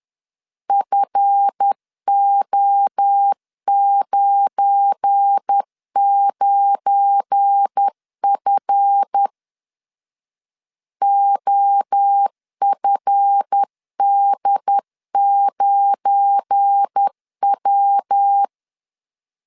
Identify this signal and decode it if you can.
Morse code, heard as a keyed tone, about 10 wpm, 800 Hz. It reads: FO99F OFD9W
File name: LEZIONI / 7 GF9Z / 7 GF9 10wpm 2747